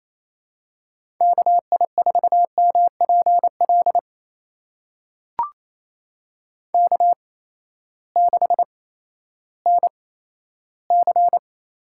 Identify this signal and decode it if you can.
Morse code, heard as a keyed tone, about 27 wpm, spaced as for 14 wpm, 700 Hz. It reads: KI4MPL K6NC
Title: USA Call Signs - 8x Character Spacing - 28wpm